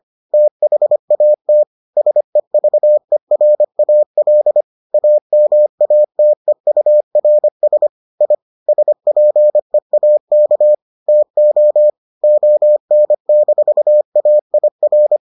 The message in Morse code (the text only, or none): THAT SEVERAL AMATEURS I SPEAK TO ON-AIR